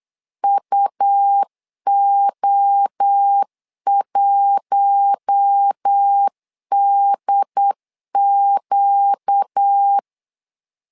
Morse code, heard as a keyed tone, about 8 wpm, 800 Hz. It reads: UO1DQ
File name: LEZIONI / 4 Q1 8wpm 3035